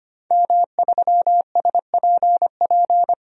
Morse code, heard as a keyed tone, about 25 wpm, 700 Hz. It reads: M3SPP
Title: DXCC-100 Call Signs - 5 Characters - 25wpm